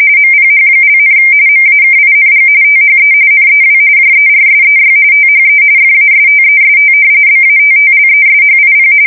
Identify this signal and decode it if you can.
RTTY, about 45.45 baud. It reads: MPH), MAKING THE NEW JERSEY THE FASTEST BATTLESHIP
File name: fsk170 / battleshipnj